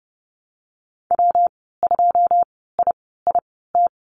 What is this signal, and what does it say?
Morse code, heard as a keyed tone, about 30 wpm, spaced as for 17 wpm, 700 Hz. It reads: W2IIT
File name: Lesson 19g - Letter w - Call Signs 30wpm